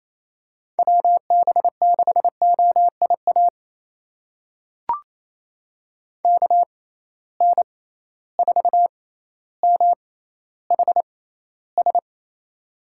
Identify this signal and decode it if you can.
Morse code, heard as a keyed tone, about 28 wpm, spaced as for 14 wpm, 700 Hz. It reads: WB6OIA KN4MHS
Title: USA Call Signs - 6 Characters - 6x Character Spacing - 28wpm